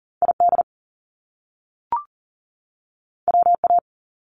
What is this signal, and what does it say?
Morse code, heard as a keyed tone, about 41 wpm, 700 Hz. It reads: ID WA
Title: US States 40wpm